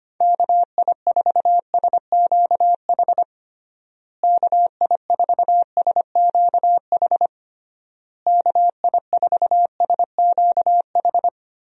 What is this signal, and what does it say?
Morse code, heard as a keyed tone, about 25 wpm, 700 Hz. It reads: KI4SQH KI4SQH KI4SQH
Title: USA Call Signs - 6 Characters - Repeated 3x - 25wpm